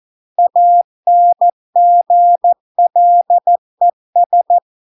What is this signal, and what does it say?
Morse code, heard as a keyed tone, about 14 wpm, 700 Hz. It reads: ANGLES